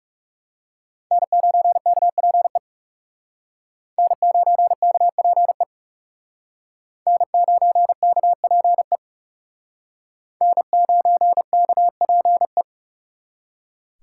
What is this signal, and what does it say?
Morse code, heard as a keyed tone, about 38 wpm, 700 Hz. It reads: N9KPE N9KPE N9KPE N9KPE